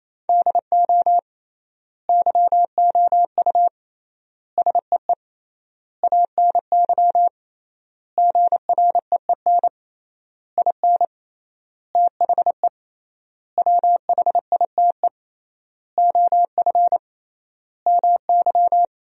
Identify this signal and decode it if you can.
Morse code, heard as a keyed tone, about 27 wpm, 700 Hz. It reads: DO YOU SEE ANY GREEN IN THE WHITE OF MY
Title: 3x Word-Spacing - Sentences from Top 500 Words - 28wpm